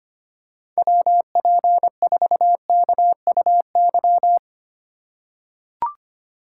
Morse code, heard as a keyed tone, about 25 wpm, 700 Hz. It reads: WP4KUY
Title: USA Call Signs - 2x Character Spacing - 25wpm